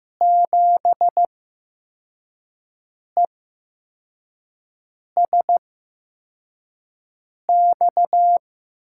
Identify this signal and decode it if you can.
Morse code, heard as a keyed tone, about 15 wpm, spaced as for 4 wpm, 700 Hz. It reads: 7ESX
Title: USA Call Signs - 5 Characters - 8x Character Spacing - 15wpm